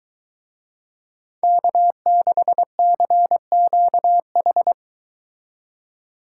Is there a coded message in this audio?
Yes: Morse code from 1.4 to 4.7 s, heard as a keyed tone, about 23 wpm, 700 Hz. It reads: K6CQH